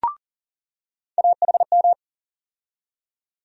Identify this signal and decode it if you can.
Morse code, heard as a keyed tone, about 40 wpm, 700 Hz. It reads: AHM
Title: USA Call Sign Postfixes - 3 Letters 40wpm